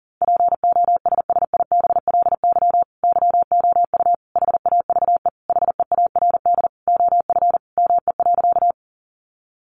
Morse code, heard as a keyed tone, about 40 wpm, 700 Hz. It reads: POSSIBLY YOU HAVE HEARD OF ME.
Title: Sentences from Top 1300 Words 40wpm